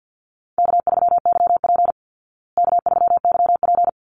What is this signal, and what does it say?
Morse code, heard as a keyed tone, about 50 wpm, 700 Hz. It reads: K3YP K3YP
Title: Call Signs Repeated 2x - 50wpm